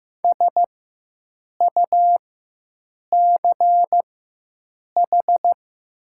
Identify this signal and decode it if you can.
Morse code, heard as a keyed tone, about 15 wpm, spaced as for 7 wpm, 700 Hz. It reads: SUCH
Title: 4x Character Spacing - Top 300 Words - 15wpm